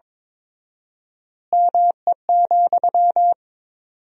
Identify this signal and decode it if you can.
Morse code, heard as a keyed tone, about 22 wpm, 700 Hz. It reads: ME,
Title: 4x Word-Spacing - Sentences from Top 200 Words - 22wpm